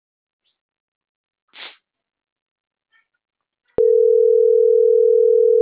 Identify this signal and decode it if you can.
Ringback tone (North America), 440 + 480 Hz.
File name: CENTRAL MABE / out edwinomarojeda - .14783